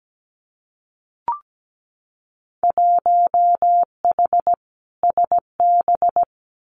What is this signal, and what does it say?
Morse code, heard as a keyed tone, about 17 wpm, 700 Hz. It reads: 1H SB
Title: Winter Field Day Exchanges 17wpm